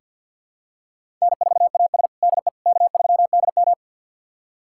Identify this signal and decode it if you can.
Morse code, heard as a keyed tone, about 47 wpm, 700 Hz. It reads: N4AS DE K3DM
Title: USA Callsign DE Callsign - 4 Characters - 50wpm